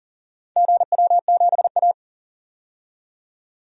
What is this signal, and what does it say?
Morse code, heard as a keyed tone, about 40 wpm, 700 Hz. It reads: GW7A
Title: DXCC-100 Call Signs - 4 Characters - 40wpm